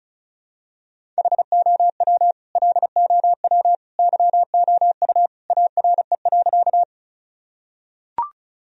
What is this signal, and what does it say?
Morse code, heard as a keyed tone, about 36 wpm, 700 Hz. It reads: HOW LOW YOU ARE.
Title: Sentences from Top 1000 Words - Review Part 1 - 35wpm